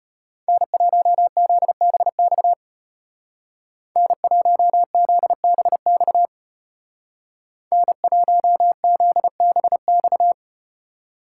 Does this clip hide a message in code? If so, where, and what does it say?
Yes: Morse code from 0.5 to 10.3 s, heard as a keyed tone, about 33 wpm, 700 Hz. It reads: N1ZBX N1ZBX N1ZBX